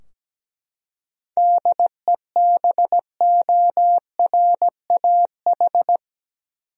Morse code, heard as a keyed tone, about 17 wpm, 700 Hz. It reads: DEBORAH
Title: USA Common Names - 17wpm